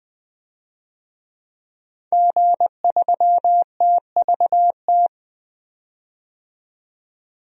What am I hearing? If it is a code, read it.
Morse code, heard as a keyed tone, about 20 wpm, 700 Hz. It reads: G3TVT
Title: DXCC-100 Call Signs - 5 Characters - 20wpm